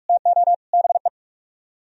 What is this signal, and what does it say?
Morse code, heard as a keyed tone, about 58 wpm, 700 Hz. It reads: TO BE